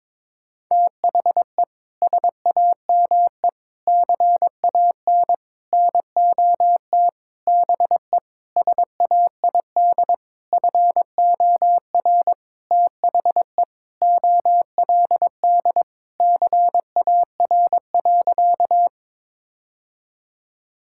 Morse code, heard as a keyed tone, about 22 wpm, 700 Hz. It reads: THE SAME CAN NOT BE SAID FOR THE OLD CAR.
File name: practice / 22wpm / Sentences with 8-Word Phrases - 22wpm